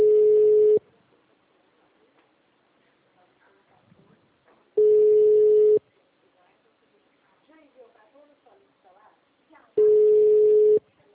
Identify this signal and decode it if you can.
Ringback tone (Europe, 425 Hz), with speech.